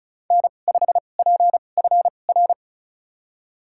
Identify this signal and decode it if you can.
Morse code, heard as a keyed tone, about 36 wpm, spaced as for 25 wpm, 700 Hz. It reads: N5PFR
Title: USA Call Signs - 5 Characters - 2x Character Spacing - 35wpm